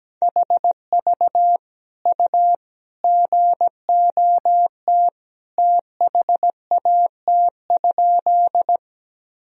Morse code, heard as a keyed tone, about 17 wpm, 700 Hz. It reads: HV U GOT THAT?
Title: Sentences from Top 300 Words and Abbreviations 17wpm